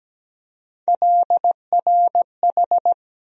Morse code, heard as a keyed tone, about 17 wpm, 700 Hz. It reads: LRH
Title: USA Call Sign Postfixes - 3 Letters 17wpm